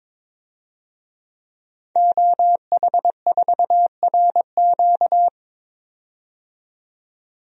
Morse code, heard as a keyed tone, about 22 wpm, 700 Hz. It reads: OH4RQ
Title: DXCC-100 Call Signs - 5 Characters - 22wpm